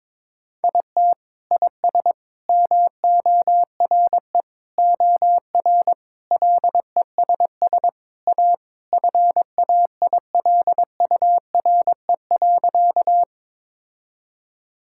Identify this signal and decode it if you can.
Morse code, heard as a keyed tone, about 21 wpm, 700 Hz. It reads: IT IS MORE OR LESS A FAILURE.
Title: Sentences from Top 1000 Words - Review Part 6 - 22wpm